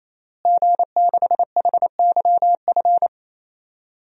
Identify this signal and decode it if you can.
Morse code, heard as a keyed tone, about 27 wpm, 700 Hz. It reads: G6HYF
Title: DXCC-100 Call Signs - 5 Characters - Repeated 3x - 28wpm